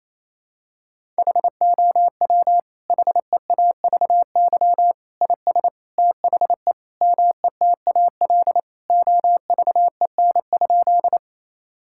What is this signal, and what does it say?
Morse code, heard as a keyed tone, about 28 wpm, 700 Hz. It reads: HOW HEAVY IS THE METAL OVEN?
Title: Sentences from Top 1000 Words - Review Part 6 - 28wpm